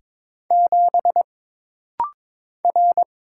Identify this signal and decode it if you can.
Morse code, heard as a keyed tone, about 21 wpm, 700 Hz. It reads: 7 R
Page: This rapid-fire timing is geared to push your ICR (Instant Character Recognition) skills.